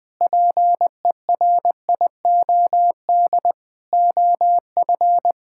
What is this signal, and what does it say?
Morse code, heard as a keyed tone, about 21 wpm, 700 Hz. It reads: PERIOD OF
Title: Sentences with 6-Word Phrases - Spoken Beginning - 20wpm